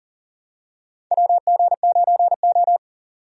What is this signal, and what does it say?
Morse code, heard as a keyed tone, about 40 wpm, 700 Hz. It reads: WG9O
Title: USA Call Signs - 4 Characters - 40wpm